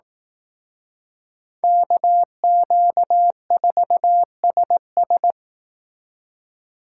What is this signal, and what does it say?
Morse code, heard as a keyed tone, about 18 wpm, 700 Hz. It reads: KQ4SS
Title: Speed Racing - USA 2x2 Call Signs 15wpm